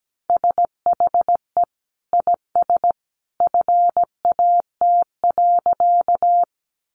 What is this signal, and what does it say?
Morse code, heard as a keyed tone, about 17 wpm, 700 Hz. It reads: SHE IS FAT.
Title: Sentences from Top 600 Words 17wpm